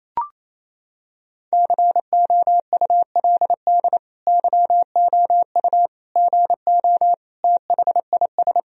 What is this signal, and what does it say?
Morse code, heard as a keyed tone, about 28 wpm, 700 Hz. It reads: COULD YOU GO THIS
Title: Sentences from Top 1000 Words - Review Part 4 - 28wpm